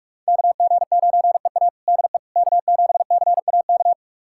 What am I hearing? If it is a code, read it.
Morse code, heard as a keyed tone, about 44 wpm, 700 Hz. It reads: KG9EA DE K7CAK
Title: USA Callsign DE Callsign - 5 Characters - 45wpm